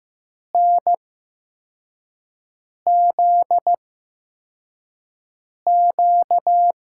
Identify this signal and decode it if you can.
Morse code, heard as a keyed tone, about 15 wpm, spaced as for 4 wpm, 700 Hz. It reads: NZQ
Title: USA Call Signs - 5 Characters - 8x Character Spacing - 15wpm